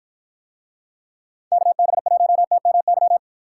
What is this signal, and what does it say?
Morse code, heard as a keyed tone, about 49 wpm, 700 Hz. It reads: KB1TMY